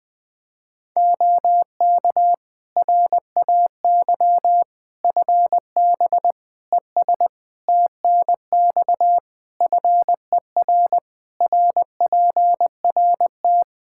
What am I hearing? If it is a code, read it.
Morse code, heard as a keyed tone, about 21 wpm, 700 Hz. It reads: OK RAY FB ES TNX FER RPRT
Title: Practice QSO - Exchange 2 - Info - 20wpm